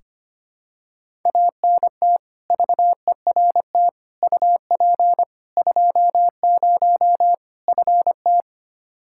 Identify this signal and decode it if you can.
Morse code, heard as a keyed tone, about 25 wpm, 700 Hz. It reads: ANT VERT UP 20 FT
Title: Practice QSO - Exchange 2 - Info - 25wpm